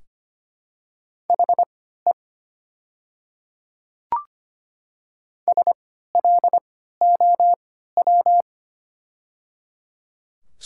Morse code, heard as a keyed tone, about 25 wpm, spaced as for 14 wpm, 700 Hz. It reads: HE SLOW
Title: 3x Character Spacing - Top 500 Words - 25wpm